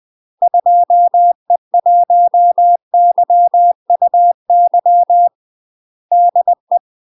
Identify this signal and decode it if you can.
Morse code, heard as a keyed tone, about 20 wpm, 700 Hz. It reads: 2E1YUY DE
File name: UK MASTER QSOs TRAINING / 20WPM / UK MASTER QSOs 0052